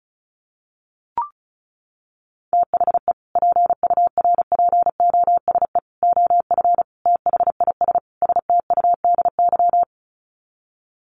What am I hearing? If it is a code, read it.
Morse code, heard as a keyed tone, about 34 wpm, 700 Hz. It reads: THE PURPOSE OF THIS STUDY
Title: Sets of 5 Words 35wpm